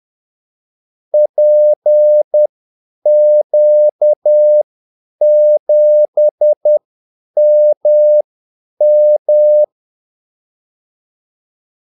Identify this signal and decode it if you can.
Morse code, heard as a keyed tone, about 10 wpm, spaced as for 8 wpm, 600 Hz. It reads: PQ7MM
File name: Callsigns-10-8wpm